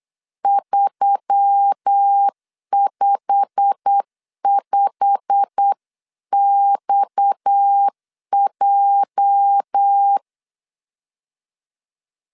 Morse code, heard as a keyed tone, about 8 wpm, 800 Hz. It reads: 355XJ
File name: CW / DET / 1 9 1500C S C 8wpm 5653